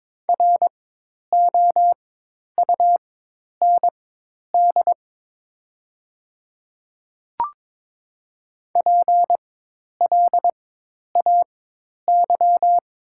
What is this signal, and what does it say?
Morse code, heard as a keyed tone, about 22 wpm, spaced as for 10 wpm, 700 Hz. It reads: ROUND PLAY
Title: 4x Character Spacing - Top 300 Words - 22wpm